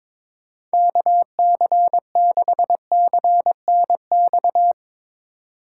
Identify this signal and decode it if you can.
Morse code, heard as a keyed tone, about 22 wpm, 700 Hz. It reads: KC6CNX
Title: USA Call Signs - 6 Characters - 3x Character Spacing - 22wpm